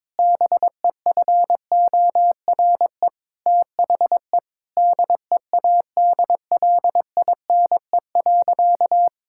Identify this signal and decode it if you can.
Morse code, heard as a keyed tone, about 22 wpm, 700 Hz. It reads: BEFORE THE DEADLINE.
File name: practice / 22wpm / Sentences with 8-Word Phrases - 22wpm